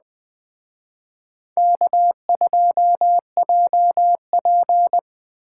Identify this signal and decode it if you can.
Morse code, heard as a keyed tone, about 20 wpm, 700 Hz. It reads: K2JP
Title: Speed Racing - USA 1x2 Call Signs 17wpm